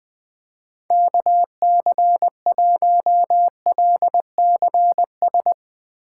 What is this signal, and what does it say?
Morse code, heard as a keyed tone, about 20 wpm, 700 Hz. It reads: KC1LCS
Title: USA Call Signs - 6 Characters - 4x Character Spacing - 20wpm